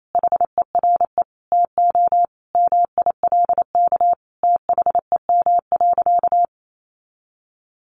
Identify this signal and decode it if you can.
Morse code, heard as a keyed tone, about 28 wpm, 700 Hz. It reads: HERE TO MILK THEM.
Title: Sentences from Top 1500 Words 28wpm